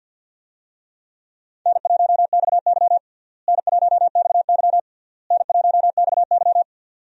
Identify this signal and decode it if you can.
Morse code, heard as a keyed tone, about 46 wpm, 700 Hz. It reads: N1XY N1XY N1XY
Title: USA Call Signs - 4 Characters - Repeated 3x - 50wpm